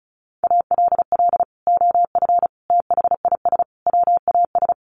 Morse code, heard as a keyed tone, about 34 wpm, 700 Hz. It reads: ALL OF THIS WAS
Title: Sentences from Top 500 Words 35wpm